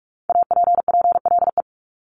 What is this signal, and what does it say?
Morse code, heard as a keyed tone, about 46 wpm, 700 Hz. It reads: APPLE
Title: Multi-letter Phonemes Repeated 3x 45wpm